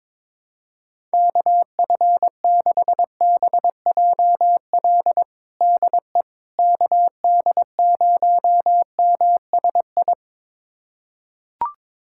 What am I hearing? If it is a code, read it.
Morse code, heard as a keyed tone, about 22 wpm, 700 Hz. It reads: KF6BJL DE KD0MSI
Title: USA Callsign DE Callsign - 6 Characters - 22wpm